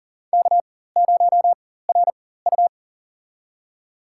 Morse code, heard as a keyed tone, about 40 wpm, spaced as for 19 wpm, 700 Hz. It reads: K0RU
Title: USA Call Signs - 4 Characters - 4x Character Spacing - 40wpm